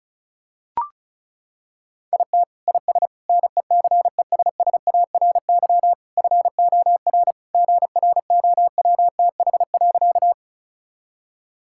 Morse code, heard as a keyed tone, about 36 wpm, 700 Hz. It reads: IT IS NECESSARY FOR GROWTH.
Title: Sentences from Top 1000 Words - Review Part 5 - 35wpm